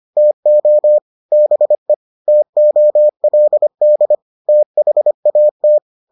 Morse code, heard as a keyed tone, about 25 wpm, 600 Hz. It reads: TO BE TOLD THAT